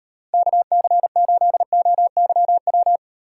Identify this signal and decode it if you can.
Morse code, heard as a keyed tone, about 38 wpm, 700 Hz. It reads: KC8OYW